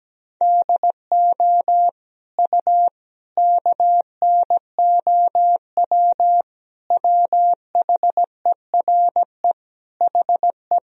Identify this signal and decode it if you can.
Morse code, heard as a keyed tone, about 17 wpm, 700 Hz. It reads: DO U KNOW WHERE HE
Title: Sentences from Top 300 Words and Abbreviations 17wpm